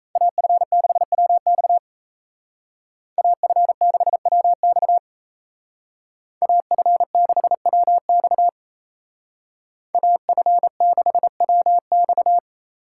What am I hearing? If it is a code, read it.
Morse code, heard as a keyed tone, about 33 wpm, 700 Hz. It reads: AF6WX AF6WX AF6WX AF6WX
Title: Speed Racing - USA 2x2 Call Signs 28wpm